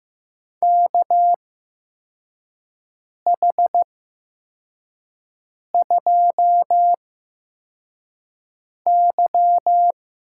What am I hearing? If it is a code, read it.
Morse code, heard as a keyed tone, about 15 wpm, spaced as for 4 wpm, 700 Hz. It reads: KH2Y
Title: USA Call Signs - 4 Characters - 8x Character Spacing - 15wpm